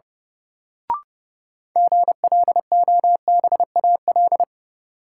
Morse code, heard as a keyed tone, about 32 wpm, 700 Hz. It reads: GLOBAL
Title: Sentences from Top 1200 Words - New Word Drill Alt - 30wpm